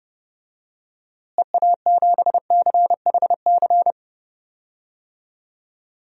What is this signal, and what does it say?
Morse code, heard as a keyed tone, about 30 wpm, 700 Hz. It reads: EA7CHC
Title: DXCC-100 Call Signs - 6 Characters - Repeated 3x - 30wpm